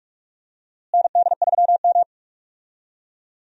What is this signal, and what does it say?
Morse code, heard as a keyed tone, about 44 wpm, 700 Hz. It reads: ND3M
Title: USA Call Signs - 4 Characters - Repeated 3x - 45wpm